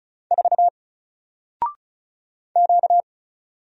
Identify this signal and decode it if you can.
Morse code, heard as a keyed tone, about 36 wpm, 700 Hz. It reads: 4 Q
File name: Single-Letter-Number-35wpm